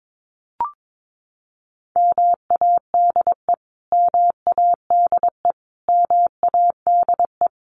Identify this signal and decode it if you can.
Morse code, heard as a keyed tone, about 22 wpm, 700 Hz. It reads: MADE MADE MADE
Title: Top 500 Words Repeated 22wpm